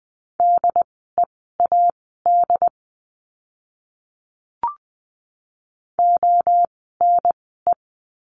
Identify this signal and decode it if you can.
Morse code, heard as a keyed tone, about 20 wpm, spaced as for 14 wpm, 700 Hz. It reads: DEAD ONE
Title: 075 Lesson 13f - Letter L - Words 20wpm